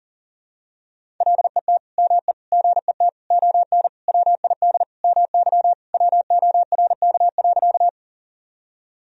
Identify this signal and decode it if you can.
Morse code, heard as a keyed tone, about 40 wpm, 700 Hz. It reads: LET ME GET ON WID MY WORK.
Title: Sentences from Top 300 Words and Abbreviations 40wpm